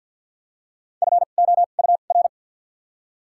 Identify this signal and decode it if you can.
Morse code, heard as a keyed tone, about 53 wpm, spaced as for 37 wpm, 700 Hz. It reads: FOUR
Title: 2x Character Spacing - Top 300 Words - 50wpm